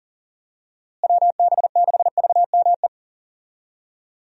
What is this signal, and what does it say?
Morse code, heard as a keyed tone, about 37 wpm, 700 Hz. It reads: WB6VME
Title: USA Call Signs - 6 Characters - Repeated 2x - 40wpm